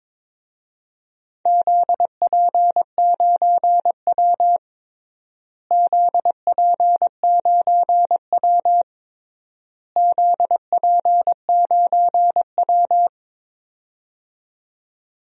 Morse code, heard as a keyed tone, about 22 wpm, 700 Hz. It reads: ZP9W ZP9W ZP9W
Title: DXCC-100 Call Signs - 4 Characters - Repeated 3x - 22wpm